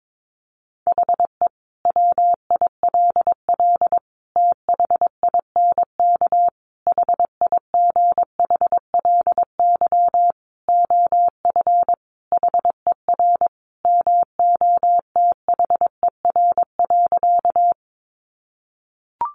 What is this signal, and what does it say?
Morse code, heard as a keyed tone, about 22 wpm, 700 Hz. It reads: HE WILL THINK HIGHLY OF HER MOTHER.
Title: Sentences from Top 1100 Words 22wpm